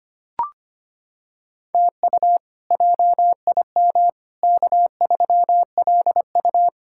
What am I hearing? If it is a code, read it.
Morse code, heard as a keyed tone, about 25 wpm, 700 Hz. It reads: TU JIM K3LU
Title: K1USN SST QSO Practice - 25wpm